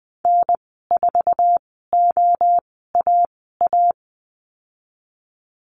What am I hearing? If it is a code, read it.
Morse code, heard as a keyed tone, about 20 wpm, spaced as for 14 wpm, 700 Hz. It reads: N4OAA
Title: 055 Lesson 10g - Letter r - Call Signs 20wpm